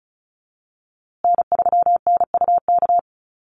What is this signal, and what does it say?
Morse code, heard as a keyed tone, about 34 wpm, 700 Hz. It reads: N3NUK
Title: USA Call Signs Repeated 2x 35wpm